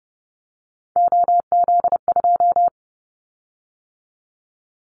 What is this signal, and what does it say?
Morse code, heard as a keyed tone, about 30 wpm, 700 Hz. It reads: OZ2
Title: Callsign Prefixes for Top 50 DXCC - 30wpm